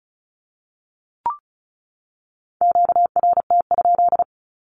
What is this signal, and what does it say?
Morse code, heard as a keyed tone, about 34 wpm, 700 Hz. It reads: QRT?
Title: Q-Codes 35wpm